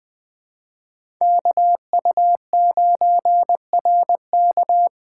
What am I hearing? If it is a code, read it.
Morse code, heard as a keyed tone, about 20 wpm, 700 Hz. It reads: KU9RK
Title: K1USN SST QSO Practice - 20wpm